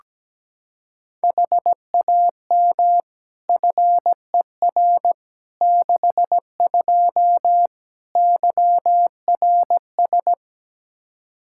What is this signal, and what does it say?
Morse code, heard as a keyed tone, about 17 wpm, 700 Hz. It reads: HAM FER 62 YRS
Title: Practice QSO - Exchange 3 - More Info - 17wpm